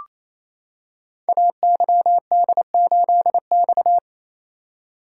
Morse code, heard as a keyed tone, about 27 wpm, 700 Hz. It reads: AYD8X
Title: Code Groups - 5 Letter:Numbers - 28wpm